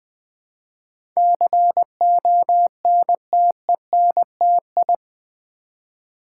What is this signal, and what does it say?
Morse code, heard as a keyed tone, about 20 wpm, 700 Hz. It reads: CONTENTI
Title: Build A Word - 4 Letters At A Time - 10-Letter Words - 20wpm